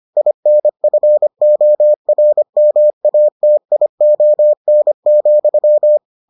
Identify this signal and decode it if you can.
Morse code, heard as a keyed tone, about 25 wpm, 600 Hz. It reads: INFORMATION,